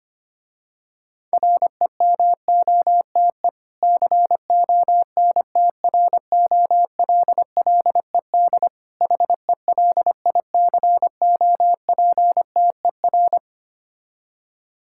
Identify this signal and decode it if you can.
Morse code, heard as a keyed tone, about 25 wpm, 700 Hz. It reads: REMOTE CONTROLLED HELICOPTER
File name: practice / 25wpm / Difficult Words with Hints - 25wpm